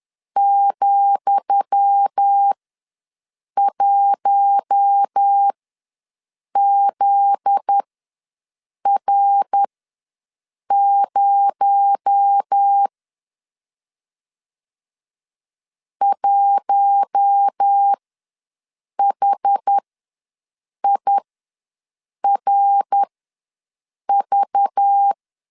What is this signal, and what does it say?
Morse code, heard as a keyed tone, about 10 wpm, spaced as for 6 wpm, 800 Hz. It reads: ,1ZR0 1HIRV
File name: LEZIONI / 10 SIGNS / 10D COMA ws 10wpm 4314